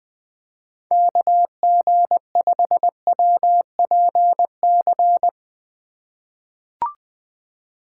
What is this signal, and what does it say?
Morse code, heard as a keyed tone, about 20 wpm, 700 Hz. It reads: KG5WPC
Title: USA Call Signs - 6 Characters - 2x Character Spacing - 20wpm